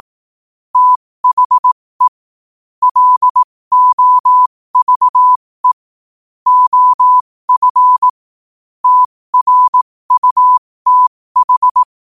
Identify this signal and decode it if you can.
Morse code, heard as a keyed tone, about 18 wpm, spaced as for 15 wpm, 1000 Hz. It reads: THE LOVE OF TRUTH
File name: qotdfiles / 15wpm